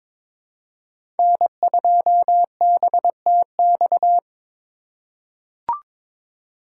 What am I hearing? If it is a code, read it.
Morse code, heard as a keyed tone, about 23 wpm, 700 Hz. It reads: N2BTX
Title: K1USN SST QSO Practice - 22wpm